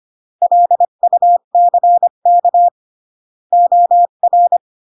Morse code, heard as a keyed tone, about 25 wpm, 700 Hz. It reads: LUCK OR
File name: CW / jodorowsky / 82 commandments 25-20